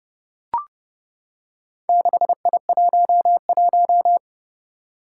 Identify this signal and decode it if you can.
Morse code, heard as a keyed tone, about 32 wpm, 700 Hz. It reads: 6I11
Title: Four Challenging Letter-Number Combinations 30wpm